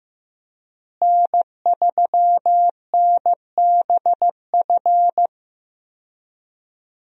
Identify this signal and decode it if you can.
Morse code, heard as a keyed tone, about 15 wpm, 700 Hz. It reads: N3NBF
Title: USA Call Signs - 5 Characters - Repeated 2x - 15wpm